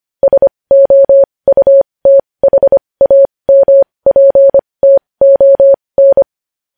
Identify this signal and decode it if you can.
Morse code, heard as a keyed tone, about 25 wpm, spaced as for 20 wpm, 550 Hz. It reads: SOUTHAMPTON
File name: podcast / News headlines-1x-25x20